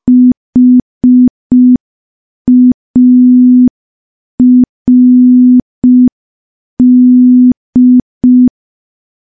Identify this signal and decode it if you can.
Morse code, heard as a keyed tone, about 5 wpm, 250 Hz. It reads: HARD
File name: morse / morse-training-05WPM